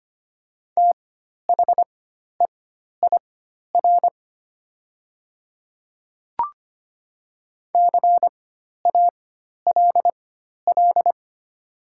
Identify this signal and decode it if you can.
Morse code, heard as a keyed tone, about 25 wpm, spaced as for 12 wpm, 700 Hz. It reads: THEIR CALL
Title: 4x Character Spacing - Top 300 Words - 25wpm